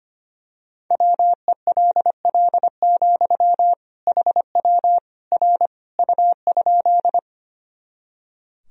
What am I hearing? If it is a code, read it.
Morse code, heard as a keyed tone, about 25 wpm, 700 Hz. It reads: WELL, HW R U?
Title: Sentences from Top 200 Words and Abbreviations 25wpm